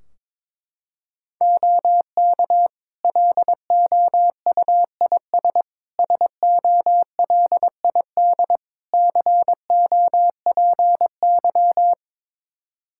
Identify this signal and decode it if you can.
Morse code, heard as a keyed tone, about 22 wpm, 700 Hz. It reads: OK LOUIS SOLID COPY
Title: Practice QSO - Exchange 3 - More Info - 22wpm